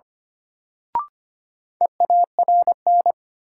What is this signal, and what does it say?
Morse code, heard as a keyed tone, about 25 wpm, 700 Hz. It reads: EARN
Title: Sentences from Top 1500 Words - New Word Drill Alt - 25wpm